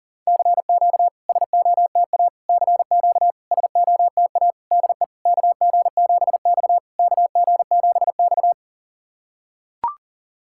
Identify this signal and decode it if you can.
Morse code, heard as a keyed tone, about 39 wpm, 690 Hz. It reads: CQ SOTA CQ SOTA DE KG7X KG7X
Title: SOTA QSO Practice 40wpm